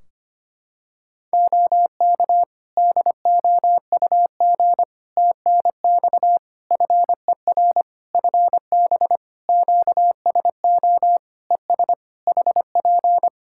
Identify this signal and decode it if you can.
Morse code, heard as a keyed tone, about 25 wpm, 700 Hz. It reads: OK DOUG TNX FER FB QSO ES HP
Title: K1USN SST QSO Practice 25wpm